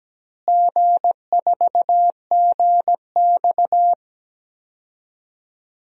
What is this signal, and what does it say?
Morse code, heard as a keyed tone, about 17 wpm, 700 Hz. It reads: G4GX
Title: DXCC-100 Call Signs - 4 Characters - Repeated 3x - 17wpm